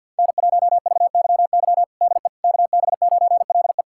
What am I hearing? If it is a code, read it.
Morse code, heard as a keyed tone, about 50 wpm, 700 Hz. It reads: N1VYY DE KB9LE
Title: USA Callsign DE Callsign - 5 Characters - 50wpm